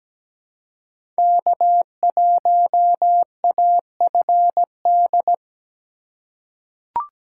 Morse code, heard as a keyed tone, about 17 wpm, 700 Hz. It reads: K1AFD
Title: USA Call Signs - 5 Characters - 2x Character Spacing - 17wpm